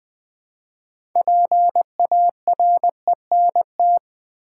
Morse code, heard as a keyed tone, about 20 wpm, 700 Hz. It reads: PARENT
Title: 6 Letter Words - First 3 Letters Given - 20wpm